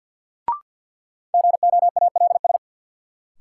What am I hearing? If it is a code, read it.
Morse code, heard as a keyed tone, about 50 wpm, 700 Hz. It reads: GOALS
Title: Sentences from Top 1000 Words - Review Part 6 - New Word Drill - 50wpm